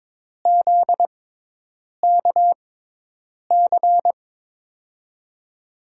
Morse code, heard as a keyed tone, about 22 wpm, spaced as for 8 wpm, 700 Hz. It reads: ZKC
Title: USA Call Signs - 5 Characters - 6x Character Spacing - 22wpm